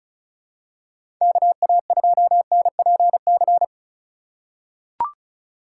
Morse code, heard as a keyed tone, about 34 wpm, 700 Hz. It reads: KA2NPC
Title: USA Call Signs - 6 Characters - 3x Character Spacing - 35wpm